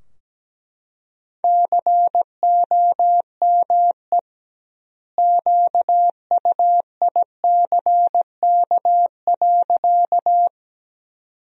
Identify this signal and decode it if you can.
Morse code, heard as a keyed tone, about 17 wpm, 700 Hz. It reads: COME QUICK.
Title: 2x Word-Spacing - Sentences from Top 500 Words - 17wpm